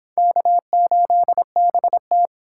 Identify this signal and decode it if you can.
Morse code, heard as a keyed tone, about 25 wpm, 700 Hz. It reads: K8BT